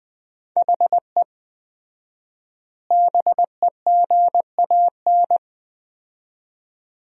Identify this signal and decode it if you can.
Morse code, heard as a keyed tone, about 20 wpm, 700 Hz. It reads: HE BEGAN
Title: 4x Word-Spacing - Sentences from Top 500 Words - 20wpm